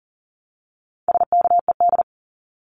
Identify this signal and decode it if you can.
Morse code, heard as a keyed tone, about 40 wpm, 700 Hz. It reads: SKED
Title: Abbreviations Repeated 3x - 40wpm